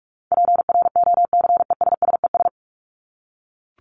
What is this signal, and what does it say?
Morse code, heard as a keyed tone, about 44 wpm, 700 Hz. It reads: PROCESSES
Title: ES:S Suffix - Top 100 Words with Base Words 45wpm